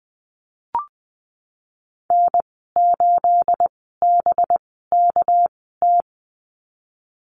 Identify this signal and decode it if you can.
Morse code, heard as a keyed tone, about 20 wpm, spaced as for 14 wpm, 700 Hz. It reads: N8BKT
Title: 219 Lesson 35g - Number 0 - Call Signs 20wpm